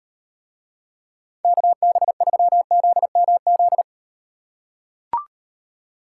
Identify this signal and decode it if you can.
Morse code, heard as a keyed tone, about 34 wpm, 700 Hz. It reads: KB3ZMZ